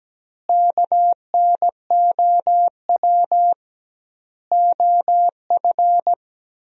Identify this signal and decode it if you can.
Morse code, heard as a keyed tone, about 17 wpm, 700 Hz. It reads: KNOW OF
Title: 2x Word-Spacing - Sentences from Top 100 Words - 17wpm